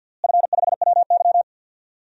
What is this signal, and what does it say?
Morse code, heard as a keyed tone, about 52 wpm, 700 Hz. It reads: F5WY
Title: DXCC-100 Call Signs - 4 Characters - Repeated 3x - 50wpm